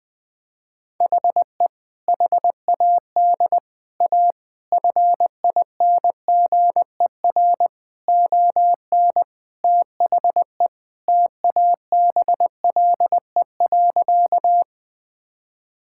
Morse code, heard as a keyed tone, about 20 wpm, 700 Hz. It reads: HE HAD A FINGER ON THE TABLE.
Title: Sentences from Top 1600 Words 20wpm